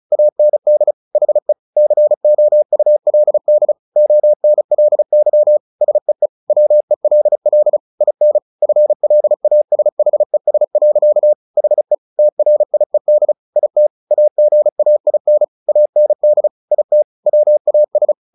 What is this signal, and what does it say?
Morse code, heard as a keyed tone, about 34 wpm, 600 Hz. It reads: AND HE COULD ONLY SEE WELL IN FLASHES. HE TRIED IT AGAIN AND IT WAS
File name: the old man and the sea 2 35wpm